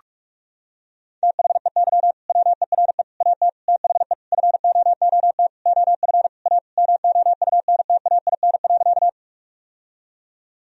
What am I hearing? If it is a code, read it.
Morse code, heard as a keyed tone, about 46 wpm, 700 Hz. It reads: THEY WERE AT THE FOOT OF A MOUNTAIN.
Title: Sentences from Top 1000 Words - Review Part 3 - 45wpm